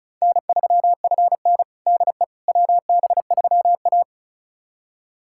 Morse code, heard as a keyed tone, about 36 wpm, 700 Hz. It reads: N3FN DE WB3A
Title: USA Callsign DE Callsign - 4 Characters - 35wpm